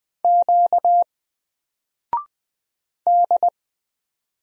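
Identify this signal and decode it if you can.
Morse code, heard as a keyed tone, about 20 wpm, 700 Hz. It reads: Q D
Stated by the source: Single letter and number Morse drill at 20 wpm, from N4LCD.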